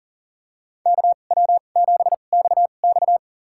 Morse code, heard as a keyed tone, about 40 wpm, spaced as for 29 wpm, 700 Hz. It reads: KW7XX
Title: USA Call Signs - 5 Characters - 2x Character Spacing - 40wpm